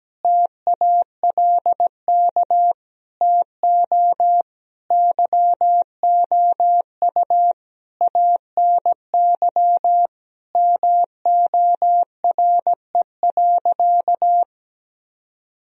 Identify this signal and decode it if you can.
Morse code, heard as a keyed tone, about 17 wpm, 700 Hz. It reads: TALK TO YOU ANY MORE.
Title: Sentences from Top 1000 Words - Review Part 1 - 17wpm